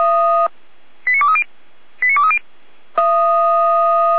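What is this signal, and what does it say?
Five-tone selective call (ZVEI), heard as quick runs of five tones, tones of about 70 ms. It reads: 89230 89230
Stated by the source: ZVEI five-tone selective call received by SDR.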